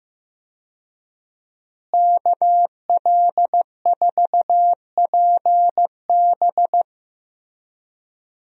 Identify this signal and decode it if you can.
Morse code, heard as a keyed tone, about 15 wpm, 700 Hz. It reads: KL4PB
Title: USA Call Signs - 5 Characters - Repeated 3x - 15wpm